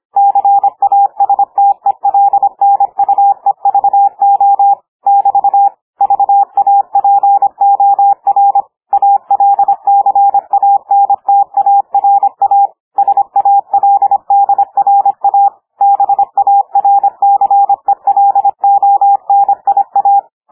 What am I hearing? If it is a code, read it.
Morse code, heard as a keyed tone, about 26 wpm, 800 Hz. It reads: CASTELNUEVO <BT> VAPOR ALCANTARA SALDRA BARCELONIA
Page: Fast hand speed - over 25 wpm